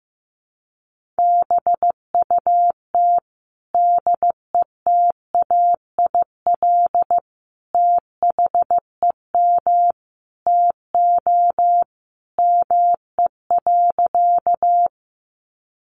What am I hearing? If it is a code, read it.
Morse code, heard as a keyed tone, about 15 wpm, 700 Hz. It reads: BUT DETAIL THEM TO ME.
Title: Sentences from Top 1500 Words 15wpm